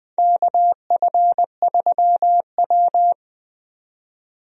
Morse code, heard as a keyed tone, about 20 wpm, 700 Hz. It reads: KF3W
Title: USA Call Signs - 4 Characters - 20wpm